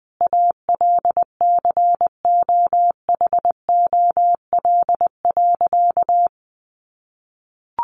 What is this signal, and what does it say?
Morse code, heard as a keyed tone, about 20 wpm, 700 Hz. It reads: ALCOHOL.
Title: Sentences from Top 1400 Words 20wpm